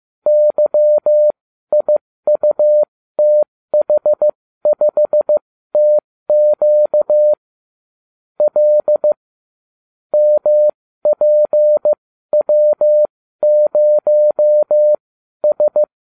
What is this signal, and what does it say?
Morse code, heard as a keyed tone, about 15 wpm, spaced as for 12 wpm, 600 Hz. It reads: YIUTH5TQ L MPW0S
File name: cw / ubacw30 3